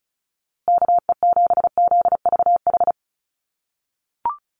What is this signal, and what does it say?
Morse code, heard as a keyed tone, about 34 wpm, 700 Hz. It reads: KE7ZVH
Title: USA Call Signs Repeated 3x 35wpm